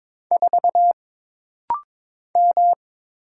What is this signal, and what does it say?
Morse code, heard as a keyed tone, about 22 wpm, 700 Hz. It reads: 4 M
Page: This rapid-fire timing is geared to push your ICR (Instant Character Recognition) skills.